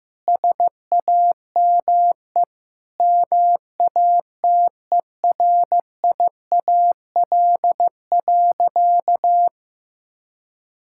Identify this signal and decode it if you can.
Morse code, heard as a keyed tone, about 16 wpm, 700 Hz. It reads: SAME MATERIAL.
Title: Sentences from Top 1000 Words - Review Part 5 - 15wpm